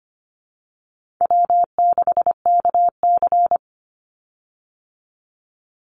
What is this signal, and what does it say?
Morse code, heard as a keyed tone, about 25 wpm, 700 Hz. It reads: W6KC
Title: Easy Call Signs Repeated 3x - 25wpm